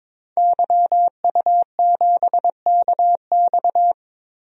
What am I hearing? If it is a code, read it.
Morse code, heard as a keyed tone, about 22 wpm, 700 Hz. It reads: YU7KX
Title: DXCC-100 Call Signs - 5 Characters - 22wpm